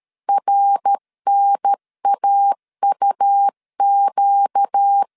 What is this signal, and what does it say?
Morse code, heard as a keyed tone, about 13 wpm, 800 Hz. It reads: RNAUQ
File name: LEZIONI / 7 GF9Z / 7 G 12wpm 2055